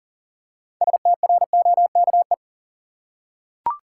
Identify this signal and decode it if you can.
Morse code, heard as a keyed tone, about 40 wpm, 700 Hz. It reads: STROKE
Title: Six Letter Words - 40wpm